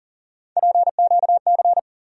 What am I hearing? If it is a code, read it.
Morse code, heard as a keyed tone, about 40 wpm, 700 Hz. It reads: PQC
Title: USA Call Sign Postfixes - 3 Letters 40wpm